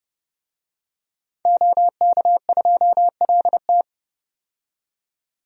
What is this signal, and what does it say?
Morse code, heard as a keyed tone, about 30 wpm, 700 Hz. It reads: OK2LT
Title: DXCC-100 Call Signs - 5 Characters - 30wpm